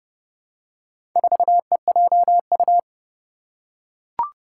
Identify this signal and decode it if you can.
Morse code, heard as a keyed tone, about 30 wpm, 700 Hz. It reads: 4EJU
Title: Four Challenging Letter-Number Combinations 30wpm